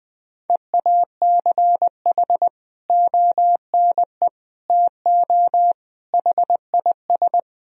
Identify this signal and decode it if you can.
Morse code, heard as a keyed tone, about 20 wpm, 700 Hz. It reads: EACH ONE TO HIS
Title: Sentences from Top 1000 Words - Review Part 4 - 20wpm